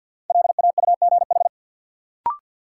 Morse code, heard as a keyed tone, about 53 wpm, 700 Hz. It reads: LAUGH
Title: Sentences from Top 500 Words - New Word Drill - 50wpm